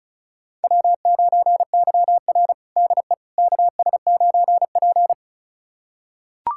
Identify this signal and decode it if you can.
Morse code, heard as a keyed tone, about 34 wpm, 700 Hz. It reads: W9YR DE KS9P
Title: USA Callsign DE Callsign - 4 Characters - 35wpm